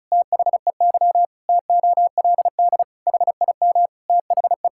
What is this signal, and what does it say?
Morse code, heard as a keyed tone, about 34 wpm, 700 Hz. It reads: THEY TOLD HIM THE
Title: Pangram Challenge v2 - 35wpm